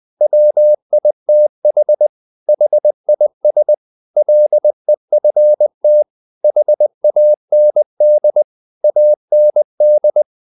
Morse code, heard as a keyed tone, about 20 wpm, 600 Hz. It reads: WITH HIS LEFT HAND AND
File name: the old man and the sea part 2 20wpm